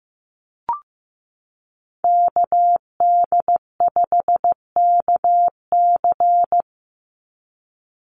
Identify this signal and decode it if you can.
Morse code, heard as a keyed tone, about 15 wpm, 700 Hz. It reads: KD5KC
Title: NA SOTA Activator Callsigns 15wpm